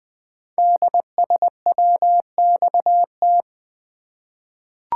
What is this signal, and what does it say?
Morse code, heard as a keyed tone, about 20 wpm, 700 Hz. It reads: DSWXT
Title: Code Groups - 5 Letters - 20wpm